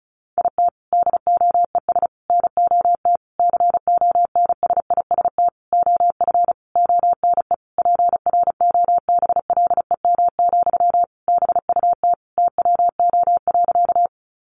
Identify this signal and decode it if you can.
Morse code, heard as a keyed tone, about 34 wpm, 700 Hz. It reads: IT DOES NOT CONSIST OF ONE PROBLEM, BUT TWO.
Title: Sentences from Top 1400 Words 35wpm